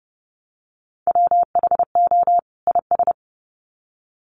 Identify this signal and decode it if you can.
Morse code, heard as a keyed tone, about 30 wpm, 700 Hz. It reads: WHO IS
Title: Sets of 2 Words 30wpm